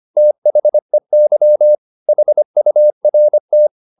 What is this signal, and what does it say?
Morse code, heard as a keyed tone, about 25 wpm, 600 Hz. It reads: THEY HURT
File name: the old man and the sea part 1 25wpm